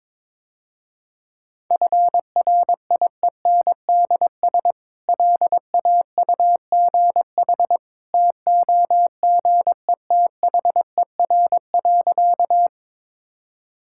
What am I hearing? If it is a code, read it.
Morse code, heard as a keyed tone, about 22 wpm, 700 Hz. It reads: FRIENDS LAUGH TOGETHER.
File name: practice / 22wpm / Easy Sentences with Hints - 22wpm